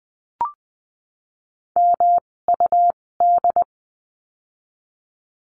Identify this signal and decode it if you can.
Morse code, heard as a keyed tone, about 20 wpm, spaced as for 16 wpm, 700 Hz. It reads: MUD
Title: 256 Lesson 42a - Words 12-20wpm